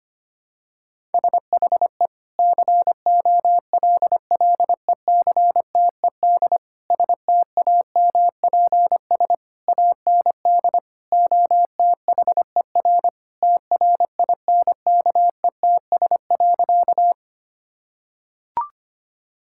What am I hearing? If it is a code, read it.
Morse code, heard as a keyed tone, about 25 wpm, 700 Hz. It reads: SHE COLLECTED STAMPS AND OTHER TRINKETS.
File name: practice / 25wpm / Sentences with 2-Word Phrases - 25wpm